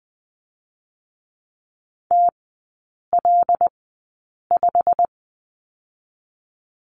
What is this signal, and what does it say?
Morse code, heard as a keyed tone, about 20 wpm, spaced as for 8 wpm, 700 Hz. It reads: TL5
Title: 087 Lesson 15e - Number 5 - Three Characters 20wpm